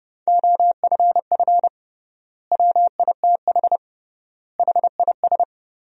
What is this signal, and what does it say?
Morse code, heard as a keyed tone, about 30 wpm, 700 Hz. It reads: OFF WITH HIS
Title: 3x Word-Spacing - Sentences from Top 300 Words - 30wpm